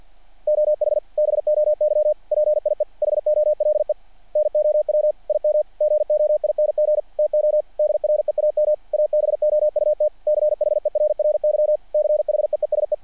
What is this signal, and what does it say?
Morse code, heard as a keyed tone, about 50 wpm, 600 Hz. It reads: OH BOY PIE HOLE NOW IM GOING TO DREAM ABOUT CHERRY CHEESE
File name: archived / CRSnet